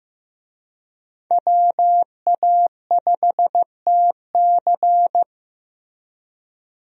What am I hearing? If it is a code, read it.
Morse code, heard as a keyed tone, about 15 wpm, 700 Hz. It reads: WA5TC
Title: USA Call Signs - 5 Characters - Repeated 3x - 15wpm